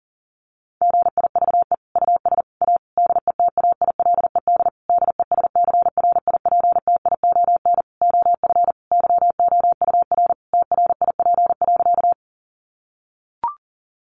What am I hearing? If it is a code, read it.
Morse code, heard as a keyed tone, about 40 wpm, 700 Hz. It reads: GIVE US A DETAILED DESCRIPTION OF YOUR TRIP.
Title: Sentences from Top 1500 Words 40wpm